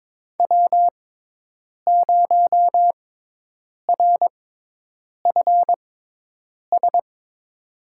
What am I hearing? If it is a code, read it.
Morse code, heard as a keyed tone, about 22 wpm, spaced as for 8 wpm, 700 Hz. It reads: W0RFS
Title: USA Call Signs - 5 Characters - 6x Character Spacing - 22wpm